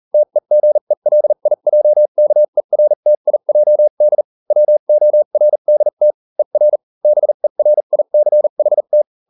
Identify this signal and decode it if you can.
Morse code, heard as a keyed tone, about 40 wpm, spaced as for 35 wpm, 600 Hz. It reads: TEGELIJKERTIJD WORDT ER BERICHT